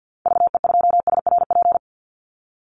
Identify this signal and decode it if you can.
Morse code, heard as a keyed tone, about 48 wpm, 700 Hz. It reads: VE2SRP
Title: Call Signs Repeated 3x - 50wpm